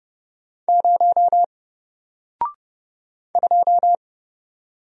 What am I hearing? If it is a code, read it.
Morse code, heard as a keyed tone, about 30 wpm, 700 Hz. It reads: 0 2
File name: Single-Letter-Number-30wpm